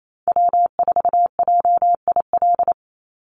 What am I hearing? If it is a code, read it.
Morse code, heard as a keyed tone, about 28 wpm, 700 Hz. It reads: W4JIL
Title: USA Call Signs 28wpm